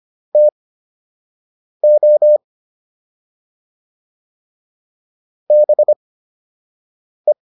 Morse code, heard as a keyed tone, about 25 wpm, spaced as for 6 wpm, 600 Hz. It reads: TO BE